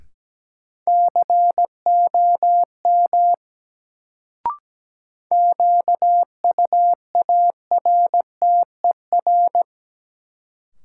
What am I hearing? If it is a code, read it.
Morse code, heard as a keyed tone, about 17 wpm, 700 Hz. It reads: COM QUARTER
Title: Sentences from Top 1300 Words - New Word Drill - 17wpm